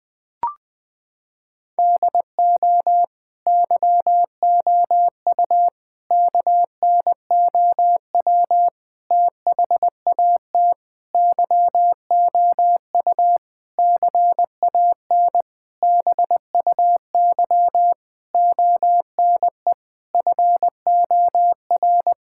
Morse code, heard as a keyed tone, about 20 wpm, 700 Hz. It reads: DO YOU KNOW THAT YOU CAN BUY ONE FOR
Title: Sentences from Top 1000 Words - Review Part 5 - 20wpm